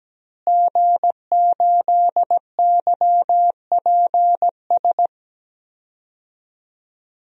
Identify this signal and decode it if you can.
Morse code, heard as a keyed tone, about 17 wpm, 700 Hz. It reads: G8YPS
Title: DXCC-100 Call Signs - 5 Characters - Repeated 2x - 17wpm